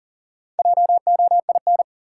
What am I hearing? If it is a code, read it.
Morse code, heard as a keyed tone, about 40 wpm, 700 Hz. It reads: JOIN
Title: Sentences from Top 1000 Words - Review Part 6 - New Word Drill - 40wpm